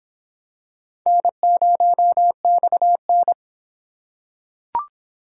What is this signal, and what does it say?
Morse code, heard as a keyed tone, about 24 wpm, 700 Hz. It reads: N0XN